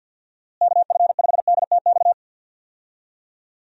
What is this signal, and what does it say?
Morse code, heard as a keyed tone, about 50 wpm, 700 Hz. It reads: KF5DTX
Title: USA Call Signs - 6 Characters - Repeated 3x - 50wpm